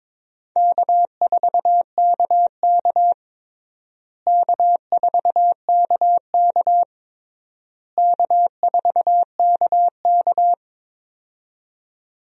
Morse code, heard as a keyed tone, about 21 wpm, 700 Hz. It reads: K4KK K4KK K4KK
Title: USA Call Signs - 4 Characters - Repeated 3x - 22wpm